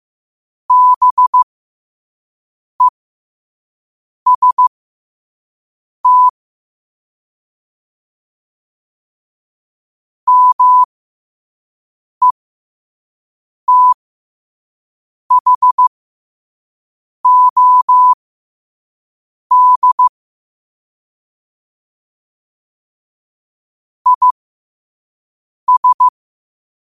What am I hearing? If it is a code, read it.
Morse code, heard as a keyed tone, about 15 wpm, spaced as for 5 wpm, 1000 Hz. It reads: BEST METHOD IS
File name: qotdfiles / 5wpm